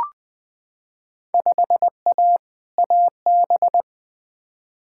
Morse code, heard as a keyed tone, about 21 wpm, 700 Hz. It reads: 5A AB
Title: Field Day Exchanges Encore - 20wpm